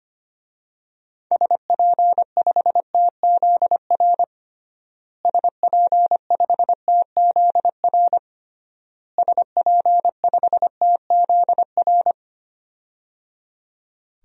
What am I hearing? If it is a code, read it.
Morse code, heard as a keyed tone, about 25 wpm, 700 Hz. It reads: SP5TZR SP5TZR SP5TZR
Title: DXCC-100 Call Signs - 6 Characters - Repeated 3x - 25wpm